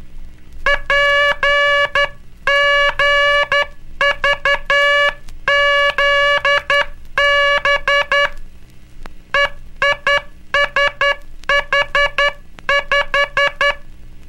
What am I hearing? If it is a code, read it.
Morse code, heard as a keyed tone, about 12 wpm, 1700 Hz, with noise. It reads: PGVZB EISH5
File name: cursos / Curso CW mp3 / CW Aula - 17